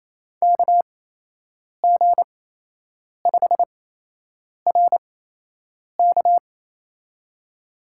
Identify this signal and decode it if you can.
Morse code, heard as a keyed tone, about 28 wpm, spaced as for 8 wpm, 700 Hz. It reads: KG5RK
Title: USA Call Signs - 8x Character Spacing - 28wpm